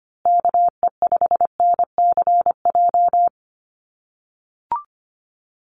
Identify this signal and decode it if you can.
Morse code, heard as a keyed tone, about 25 wpm, 700 Hz. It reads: KE5NCJ
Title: USA Call Signs Repeated 3x 25wpm